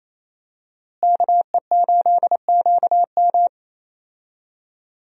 Morse code, heard as a keyed tone, about 28 wpm, 700 Hz. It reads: KE8QM
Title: USA Call Signs - 5 Characters - Repeated 3x - 28wpm